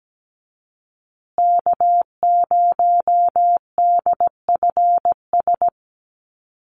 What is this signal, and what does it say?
Morse code, heard as a keyed tone, about 17 wpm, 700 Hz. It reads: K0DFS
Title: USA Call Signs Repeated 3x 17wpm